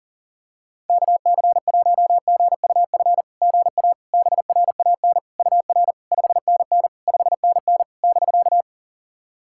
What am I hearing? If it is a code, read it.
Morse code, heard as a keyed tone, about 40 wpm, 690 Hz. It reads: KC1GUF GA BRAN UR 5NN 5NN <BK>
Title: SOTA QSO Practice 40wpm